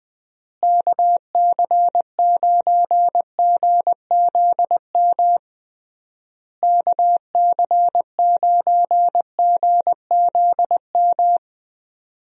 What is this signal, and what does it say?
Morse code, heard as a keyed tone, about 20 wpm, 700 Hz. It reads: KC9GZM KC9GZM
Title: USA Call Signs - 6 Characters - Repeated 3x - 20wpm